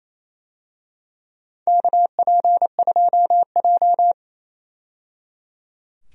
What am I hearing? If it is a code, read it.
Morse code, heard as a keyed tone, about 28 wpm, 700 Hz. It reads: KP2J
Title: DXCC-100 Call Signs - 4 Characters - 28wpm